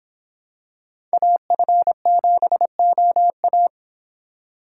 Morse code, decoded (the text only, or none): AF7OA